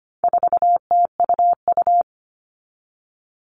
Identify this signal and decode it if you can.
Morse code, heard as a keyed tone, about 25 wpm, 700 Hz. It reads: 4TUU
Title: All Cut Numbers - 4-Digit 25wpm